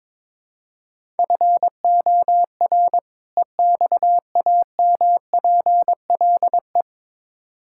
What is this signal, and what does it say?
Morse code, heard as a keyed tone, about 22 wpm, 700 Hz. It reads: FOR EXAMPLE
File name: Build A Phrase - 2-Word Phrases - 22wpm